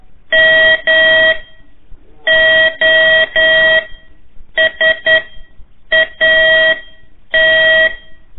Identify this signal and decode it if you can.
Morse code, heard as a keyed tone, about 11 wpm, spaced as for 7 wpm, 1900 Hz. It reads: MOSAT
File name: 3 lezione CW 1 AIR